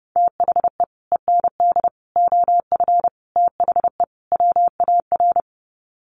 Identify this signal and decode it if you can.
Morse code, heard as a keyed tone, about 30 wpm, 700 Hz. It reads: THE END OF THE WAR
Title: Sets of 5 Words 30wpm